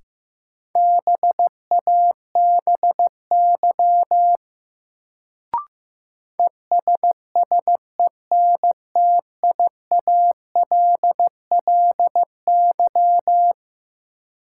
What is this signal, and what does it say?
Morse code, heard as a keyed tone, about 16 wpm, 700 Hz. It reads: BABY ESSENTIALLY
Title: Sentences from Top 1300 Words - New Word Drill - 15wpm